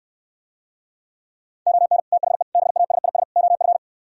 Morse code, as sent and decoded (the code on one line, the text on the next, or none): --- -. / - .... / -... .- ... .. ... / --- ..-.
ON THE BASIS OF